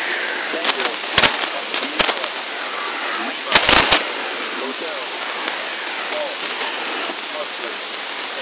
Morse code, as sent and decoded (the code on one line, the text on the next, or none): none